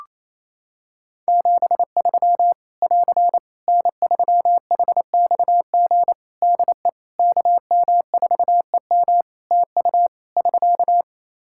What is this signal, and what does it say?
Morse code, heard as a keyed tone, about 27 wpm, 700 Hz. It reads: 73 <AR> N3HXG DE KM4EM TU <SK>
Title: Practice QSO - Exchange 4 - Ending - 28wpm